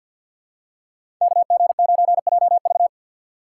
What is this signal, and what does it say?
Morse code, heard as a keyed tone, about 52 wpm, 700 Hz. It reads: KG9JV
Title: USA Call Signs - 5 Characters - 6x Character Spacing - 50wpm